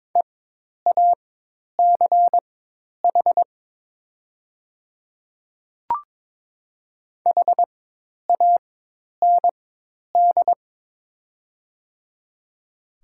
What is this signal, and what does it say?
Morse code, heard as a keyed tone, about 22 wpm, spaced as for 10 wpm, 700 Hz. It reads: EACH HAND
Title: 4x Character Spacing - Top 200 Words - 22wpm